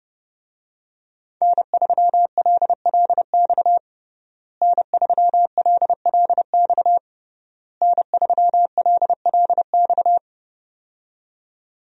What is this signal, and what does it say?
Morse code, heard as a keyed tone, about 30 wpm, 700 Hz. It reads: N3LLX N3LLX N3LLX
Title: USA Call Signs - 5 Characters - Repeated 3x - 30wpm